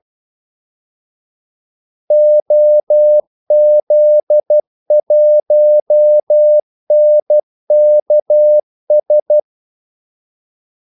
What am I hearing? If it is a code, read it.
Morse code, heard as a keyed tone, about 12 wpm, 600 Hz. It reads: OZ1NKS
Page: 2 ekstra ordmellemrum mellem kaldesignalerne